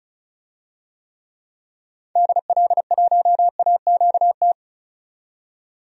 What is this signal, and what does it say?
Morse code, heard as a keyed tone, about 36 wpm, 700 Hz. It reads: DL1AQT
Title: DXCC-100 Call Signs - 6 Characters - 35wpm